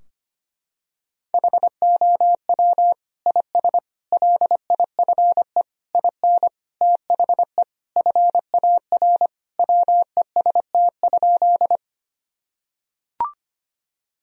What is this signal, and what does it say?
Morse code, heard as a keyed tone, about 25 wpm, 700 Hz. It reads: HOW IS LIFE IN THE FAR WEST?
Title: Sentences from Top 1000 Words - Review Part 2 - 25wpm